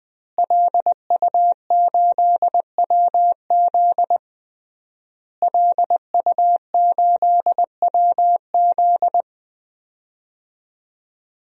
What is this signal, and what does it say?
Morse code, heard as a keyed tone, about 20 wpm, 700 Hz. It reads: LU8WZ LU8WZ
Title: DXCC-100 Call Signs - 5 Characters - Repeated 3x - 20wpm